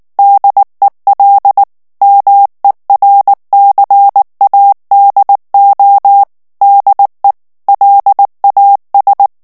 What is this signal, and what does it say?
Morse code, heard as a keyed tone, about 19 wpm, 800 Hz. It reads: DEL MERCADO DE LAS
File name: morse mp3 / ESPAGNOL TEXTE EN CLAIR 2 VITESSE 18 MOTS MINUTES